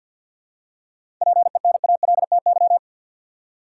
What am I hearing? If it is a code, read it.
Morse code, heard as a keyed tone, about 53 wpm, 700 Hz. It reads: PENALTY
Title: Seven Letter Words - 50wpm